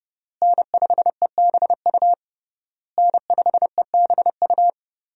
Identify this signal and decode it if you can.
Morse code, heard as a keyed tone, about 32 wpm, 700 Hz. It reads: N5EBU N5EBU
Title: USA Call Signs - 5 Characters - Repeated 3x - 30wpm